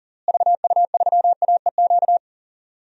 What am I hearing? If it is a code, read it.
Morse code, heard as a keyed tone, about 40 wpm, 700 Hz. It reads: VU3AEQ
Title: DXCC-100 Call Signs - 6 Characters - Repeated 3x - 40wpm